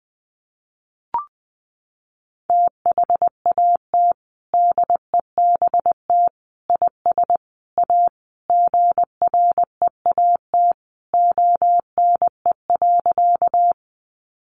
Morse code, heard as a keyed tone, about 20 wpm, 700 Hz. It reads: THAT DEBT IS A GREAT ONE.
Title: Sentences from Top 1300 Words 20wpm